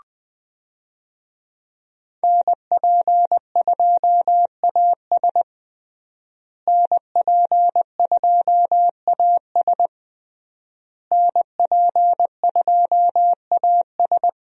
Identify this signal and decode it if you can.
Morse code, heard as a keyed tone, about 20 wpm, 700 Hz. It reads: NP2AS NP2AS NP2AS
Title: USA Call Signs - 5 Characters - Repeated 3x - 20wpm